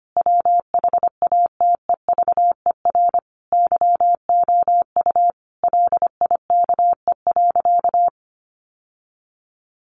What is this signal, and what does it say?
Morse code, heard as a keyed tone, about 25 wpm, 700 Hz. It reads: WHATEVER YOU LIKE.
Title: Sentences from Top 1100 Words 25wpm